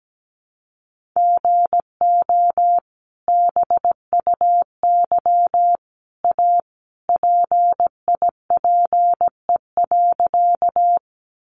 Morse code, heard as a keyed tone, about 17 wpm, 700 Hz. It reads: GO BUY A PIPE.
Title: Sentences from Top 1500 Words 17wpm